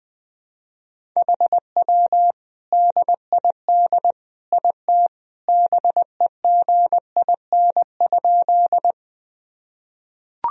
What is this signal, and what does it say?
Morse code, heard as a keyed tone, about 21 wpm, 700 Hz. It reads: HW DID IT BEGIN?
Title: Sentences from Top 300 Words and Abbreviations 20wpm